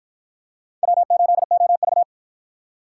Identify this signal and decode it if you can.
Morse code, heard as a keyed tone, about 49 wpm, 700 Hz. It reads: W8OV